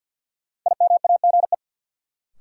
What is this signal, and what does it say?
Morse code, heard as a keyed tone, about 49 wpm, 700 Hz. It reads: IMAGE
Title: Sentences from Top 700 Words - New Word Drill - 50wpm